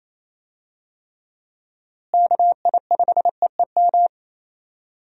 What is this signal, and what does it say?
Morse code, heard as a keyed tone, about 27 wpm, 700 Hz. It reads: KI5EEM
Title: USA Call Signs - 6 Characters - Repeated 2x - 28wpm